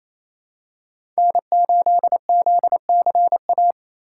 Morse code, heard as a keyed tone, about 27 wpm, 700 Hz. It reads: N8ZCA
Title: USA Call Signs - 5 Characters - 2x Character Spacing - 28wpm